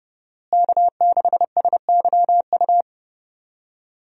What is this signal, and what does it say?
Morse code, heard as a keyed tone, about 30 wpm, 700 Hz. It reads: K6SYU
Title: USA Call Signs - 5 Characters - Repeated 2x - 30wpm